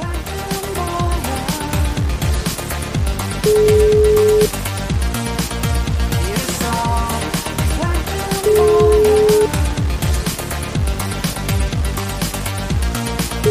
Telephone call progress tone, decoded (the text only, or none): ringback tone